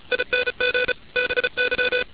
Morse code, heard as a keyed tone, about 33 wpm, 500 Hz, with noise. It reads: ING BY